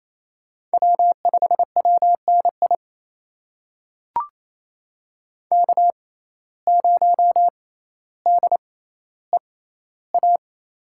Morse code, heard as a keyed tone, about 28 wpm, spaced as for 14 wpm, 700 Hz. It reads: W5WNI K0DEA
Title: USA Call Signs - 5 Characters - 6x Character Spacing - 28wpm